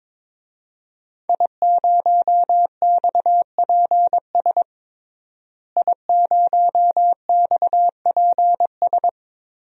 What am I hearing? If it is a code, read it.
Morse code, heard as a keyed tone, about 22 wpm, 700 Hz. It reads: I0XPS I0XPS
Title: DXCC-100 Call Signs - 5 Characters - Repeated 2x - 22wpm